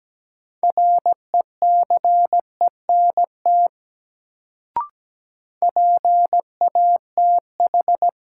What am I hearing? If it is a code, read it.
Morse code, heard as a keyed tone, about 17 wpm, 700 Hz. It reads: RECENT PATH
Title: Sentences from Top 1400 Words - New Word Drill Alt - 17wpm